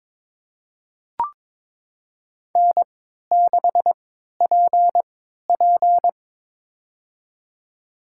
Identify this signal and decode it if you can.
Morse code, heard as a keyed tone, about 21 wpm, spaced as for 12 wpm, 700 Hz. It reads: N6PP
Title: USA Call Signs - 4 Characters - 3x Character Spacing - 22wpm